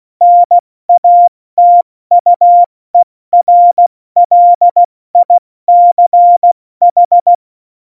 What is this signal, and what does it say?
Morse code, heard as a keyed tone, about 16 wpm, 700 Hz. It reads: NATUERLICH
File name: Morsekurs / Witze / Witz 05 Mai, 2020 70 F80